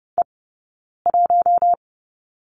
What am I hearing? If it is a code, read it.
Morse code, heard as a keyed tone, about 30 wpm, spaced as for 9 wpm, 690 Hz. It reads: E1
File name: Lesson 19d - Letter w - Two Characters 30wpm